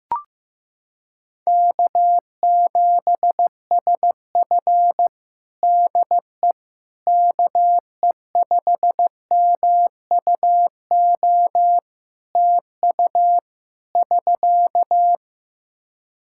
Morse code, heard as a keyed tone, about 15 wpm, 700 Hz. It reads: K7SF DE KE5MUO TU <SK>
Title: K1USN SST QSO Practice 15wpm